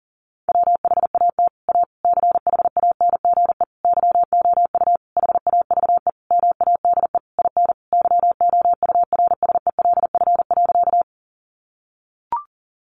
Morse code, heard as a keyed tone, about 40 wpm, 700 Hz. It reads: WHAT A CHANGE YOU HAVE MADE IN YOURSELF.
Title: Sentences from Top 600 Words 40wpm